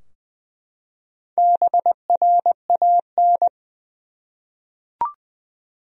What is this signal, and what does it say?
Morse code, heard as a keyed tone, about 20 wpm, 700 Hz. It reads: BRAN
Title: USA Common Names - Max 5 Letters - 20wpm